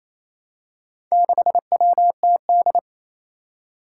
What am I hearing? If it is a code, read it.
Morse code, heard as a keyed tone, about 28 wpm, 700 Hz. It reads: 6WTD
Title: Four Challenging Letter-Number Combinations 28wpm